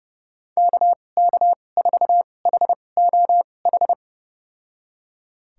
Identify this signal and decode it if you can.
Morse code, heard as a keyed tone, about 32 wpm, spaced as for 22 wpm, 700 Hz. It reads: KK4HOH
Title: USA Call Signs - 6 Characters - 2x Character Spacing - 30wpm